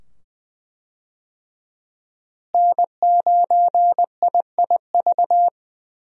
Morse code, heard as a keyed tone, about 21 wpm, 700 Hz. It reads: N9IIV
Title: USA Call Signs - 5 Characters - Repeated 2x - 20wpm